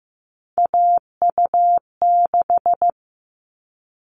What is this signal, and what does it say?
Morse code, heard as a keyed tone, about 15 wpm, 700 Hz. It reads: AU6
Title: All Cut Numbers - 3-Digit 15wpm